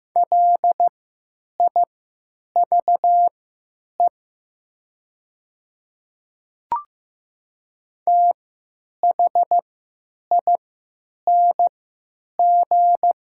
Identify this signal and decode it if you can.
Morse code, heard as a keyed tone, about 15 wpm, spaced as for 9 wpm, 700 Hz. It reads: LIVE THING
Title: 3x Character Spacing - Top 200 Words - 15wpm